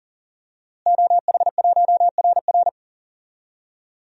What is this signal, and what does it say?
Morse code, heard as a keyed tone, about 40 wpm, 700 Hz. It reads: OH1RR
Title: DXCC-100 Call Signs - 5 Characters - Repeated 3x - 40wpm